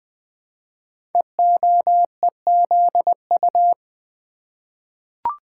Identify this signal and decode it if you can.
Morse code, heard as a keyed tone, about 20 wpm, 700 Hz. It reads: EOEZU
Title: Code Groups - 5 Letters - 20wpm